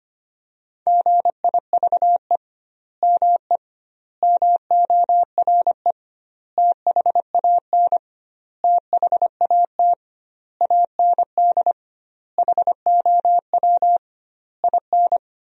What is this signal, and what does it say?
Morse code, heard as a keyed tone, about 25 wpm, 700 Hz. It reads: GIVE ME MORE THAN THAT AND HOW IN
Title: 2x Word-Spacing - Sentences from Top 200 Words - 25wpm